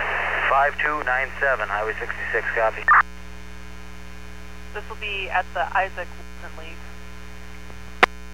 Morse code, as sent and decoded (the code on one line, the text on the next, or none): none